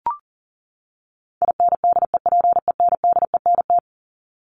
Morse code, heard as a keyed tone, about 40 wpm, 700 Hz. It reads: INDEPENDENT
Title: Top 751-1000 Words 40wpm